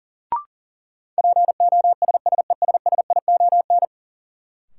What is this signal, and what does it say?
Morse code, heard as a keyed tone, about 37 wpm, 700 Hz. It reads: POSSESSION
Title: Sentences from Top 1400 Words - New Word Drill - 40wpm